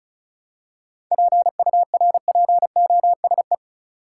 Morse code, heard as a keyed tone, about 34 wpm, 700 Hz. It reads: PURPOSE
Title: Seven Letter Words - 35wpm